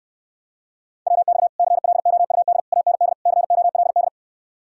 Morse code, heard as a keyed tone, about 69 wpm, 700 Hz. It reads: WX CLOUD AND COLD